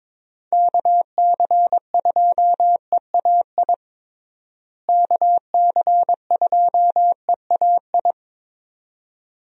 Morse code, heard as a keyed tone, about 21 wpm, 700 Hz. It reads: KC2EAI KC2EAI
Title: USA Call Signs - 6 Characters - Repeated 2x - 22wpm